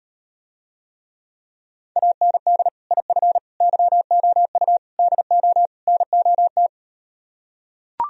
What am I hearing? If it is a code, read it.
Morse code, heard as a keyed tone, about 39 wpm, 700 Hz. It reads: AND IF YOU DO NOT